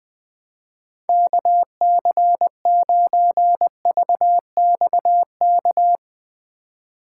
Morse code, heard as a keyed tone, about 21 wpm, 700 Hz. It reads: KC9VXK
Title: Speed Racing - USA Call Signs 20wpm